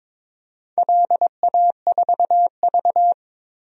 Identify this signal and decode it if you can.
Morse code, heard as a keyed tone, about 22 wpm, 700 Hz. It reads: LA4V
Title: DXCC-100 Call Signs - 4 Characters - Repeated 2x - 22wpm